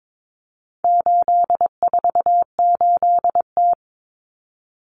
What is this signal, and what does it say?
Morse code, heard as a keyed tone, about 22 wpm, 700 Hz. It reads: 848T
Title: Common Cut Numbers - 4-Digit 22wpm